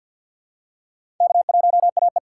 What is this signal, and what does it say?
Morse code, heard as a keyed tone, about 49 wpm, 700 Hz. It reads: K1AE
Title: USA Call Signs - 4 Characters - 6x Character Spacing - 50wpm